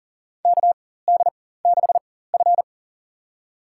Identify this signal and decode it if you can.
Morse code, heard as a keyed tone, about 40 wpm, spaced as for 19 wpm, 700 Hz. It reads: KD6F
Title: USA Call Signs - 4 Characters - 4x Character Spacing - 40wpm